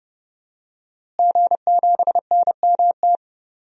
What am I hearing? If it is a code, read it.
Morse code, heard as a keyed tone, about 30 wpm, 700 Hz. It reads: G7NMT
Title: DXCC-100 Call Signs - 5 Characters - Repeated 3x - 30wpm